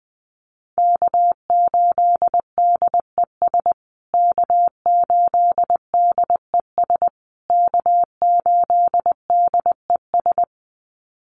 Morse code, heard as a keyed tone, about 20 wpm, 700 Hz. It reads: K8DES K8DES K8DES
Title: USA Call Signs Repeated 3x 20wpm